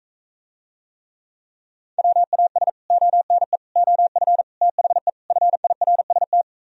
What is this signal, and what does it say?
Morse code, heard as a keyed tone, about 45 wpm, 700 Hz. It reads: WAS ONE OF THE FIRST